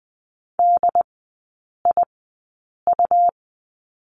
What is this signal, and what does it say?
Morse code, heard as a keyed tone, about 20 wpm, spaced as for 8 wpm, 700 Hz. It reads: DIU
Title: 100 Lesson 17e - Letter u - Three Characters 20wpm